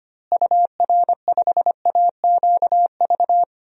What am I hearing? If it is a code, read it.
Morse code, heard as a keyed tone, about 25 wpm, 700 Hz. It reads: UR5AQV
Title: DXCC-100 Call Signs - 6 Characters - Repeated 2x - 25wpm